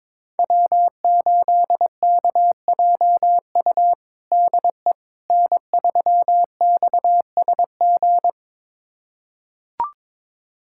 Morse code, heard as a keyed tone, about 22 wpm, 700 Hz. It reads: W8KJU DE N3XSG
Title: USA Callsign DE Callsign - 5 Characters - 22wpm